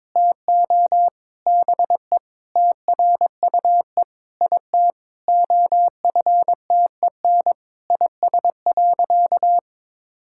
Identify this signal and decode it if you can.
Morse code, heard as a keyed tone, about 22 wpm, 700 Hz. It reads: TO BE TRUE IT OFTEN IS.
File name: Sentences with 9-Word Phrases - 22wpm